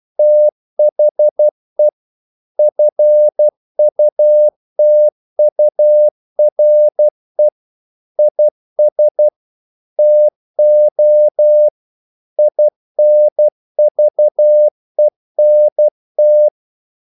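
Morse code, heard as a keyed tone, about 12 wpm, 600 Hz. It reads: THE FUTURE IS TO INVENT
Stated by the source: Morse practice sending English quotations.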